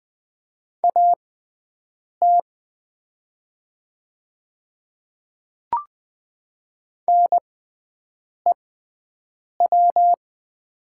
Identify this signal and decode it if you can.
Morse code, heard as a keyed tone, about 20 wpm, spaced as for 7 wpm, 700 Hz. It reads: AT NEW
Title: 6x Character Spacing - Top 200 Words - 20wpm